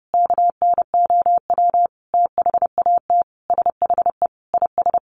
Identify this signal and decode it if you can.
Morse code, heard as a keyed tone, about 30 wpm, 700 Hz. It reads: KNOW THAT SHE IS
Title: Sentences from Top 1300 Words 30wpm